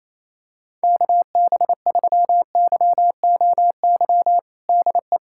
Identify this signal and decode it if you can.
Morse code, heard as a keyed tone, about 28 wpm, 700 Hz. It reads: KB3YOY DE
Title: USA Callsign DE Callsign 28wpm